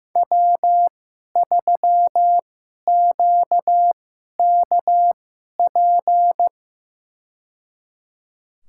Morse code, heard as a keyed tone, about 15 wpm, spaced as for 11 wpm, 700 Hz. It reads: W3QKP
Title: USA Call Signs - 5 Characters - 2x Character Spacing - 15wpm